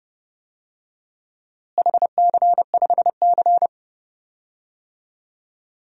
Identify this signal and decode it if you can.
Morse code, heard as a keyed tone, about 30 wpm, 700 Hz. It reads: HC5C
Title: DXCC-100 Call Signs - 4 Characters - Repeated 3x - 30wpm